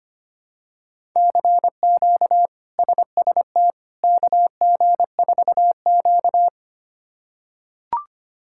Morse code, heard as a keyed tone, about 25 wpm, 700 Hz. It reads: CQ SST KG4Q
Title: K1USN SST QSO Practice - 25wpm